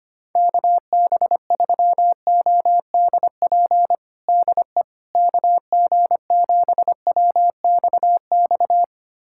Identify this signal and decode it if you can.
Morse code, heard as a keyed tone, about 25 wpm, 700 Hz. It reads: KB3ODP DE KG7WXX
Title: USA Callsign DE Callsign 25wpm